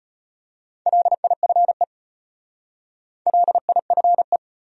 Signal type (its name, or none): Morse code